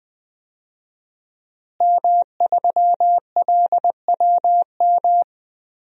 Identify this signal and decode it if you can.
Morse code, heard as a keyed tone, about 20 wpm, 700 Hz. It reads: M3LWM
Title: DXCC-100 Call Signs - 5 Characters - Repeated 2x - 20wpm